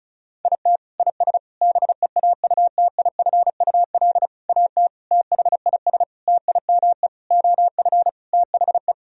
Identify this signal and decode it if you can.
Morse code, heard as a keyed tone, about 34 wpm, 700 Hz. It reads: IT IS BEAUTIFUL AT THIS TIME OF THE
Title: Sentences from Top 1000 Words - Review Part 1 - 35wpm